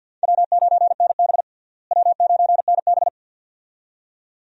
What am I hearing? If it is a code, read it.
Morse code, heard as a keyed tone, about 50 wpm, 700 Hz. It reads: W9NB W9NB
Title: USA Call Signs - 4 Characters - Repeated 2x - 50wpm